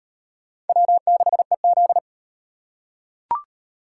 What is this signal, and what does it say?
Morse code, heard as a keyed tone, about 35 wpm, 700 Hz. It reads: W6EZ